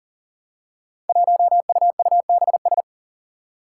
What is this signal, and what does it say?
Morse code, heard as a keyed tone, about 40 wpm, 700 Hz. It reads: 1UUBS
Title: Five Challenging Letter-Number Combinations 40wpm